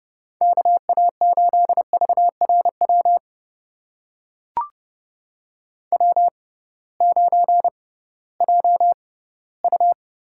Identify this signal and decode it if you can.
Morse code, heard as a keyed tone, about 30 wpm, spaced as for 18 wpm, 700 Hz. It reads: KA8VRW W9JU
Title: USA Call Signs - 6x Character Spacing - 30wpm